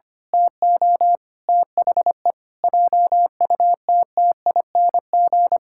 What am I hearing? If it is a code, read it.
Morse code, heard as a keyed tone, about 25 wpm, 700 Hz. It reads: TO THE JUTTING
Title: Pangram Challenge v2 - 25wpm